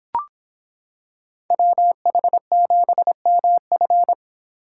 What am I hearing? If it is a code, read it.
Morse code, heard as a keyed tone, about 25 wpm, 700 Hz. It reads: WH7MF